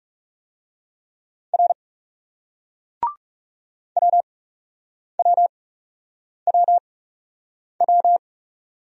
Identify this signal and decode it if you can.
Morse code, heard as a keyed tone, about 38 wpm, spaced as for 8 wpm, 700 Hz. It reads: R WWWW